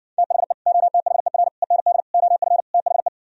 Morse code, heard as a keyed tone, about 67 wpm, 700 Hz. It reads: THE OTHER END OF THE